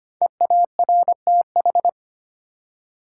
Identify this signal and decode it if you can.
Morse code, heard as a keyed tone, about 25 wpm, 700 Hz. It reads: EARTH
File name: 5 Letter Words - First 3 Letters Given - 25wpm